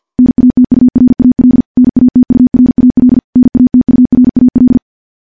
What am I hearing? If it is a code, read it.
Morse code, heard as a keyed tone, about 50 wpm, 250 Hz. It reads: NATURAL NATURAL NATURAL
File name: morse / morse-training-50WPM